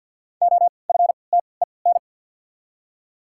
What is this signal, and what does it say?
Morse code, heard as a keyed tone, about 53 wpm, spaced as for 29 wpm, 700 Hz. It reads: OFTEN
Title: 3x Character Spacing - Top 500 Words - 50wpm